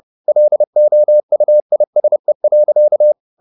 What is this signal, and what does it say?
Morse code, heard as a keyed tone, about 30 wpm, 600 Hz. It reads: LOUISE.
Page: Ferie 30 wpm